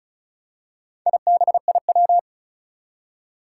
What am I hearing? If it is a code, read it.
Morse code, heard as a keyed tone, about 36 wpm, 690 Hz. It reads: IBIW
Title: Four Challenging Letter-Number Combinations 35wpm